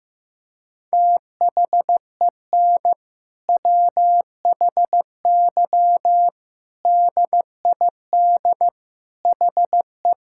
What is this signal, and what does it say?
Morse code, heard as a keyed tone, about 15 wpm, 700 Hz. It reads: THEN WHY DID HE
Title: Sentences from Top 1000 Words - Review Part 5 - 15wpm